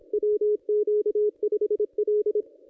Morse code, heard as a keyed tone, about 24 wpm, 440 Hz. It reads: WQ5L
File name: contests / ss-cw-k5tr-op-n5rz -1400